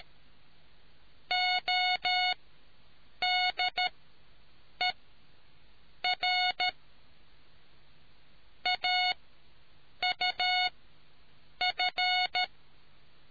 Morse code, heard as a keyed tone, about 13 wpm, spaced as for 7 wpm, 730 Hz. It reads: ODER AUF